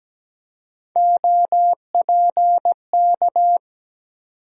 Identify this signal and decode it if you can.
Morse code, heard as a keyed tone, about 17 wpm, 700 Hz. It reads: OPK
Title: USA Call Sign Postfixes - 3 Letters 17wpm